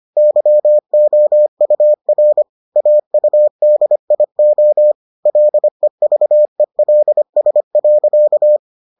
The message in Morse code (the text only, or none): YOUR AUDIO LEVELS.